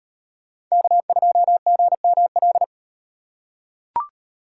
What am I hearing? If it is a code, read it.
Morse code, heard as a keyed tone, about 38 wpm, 700 Hz. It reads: K2GML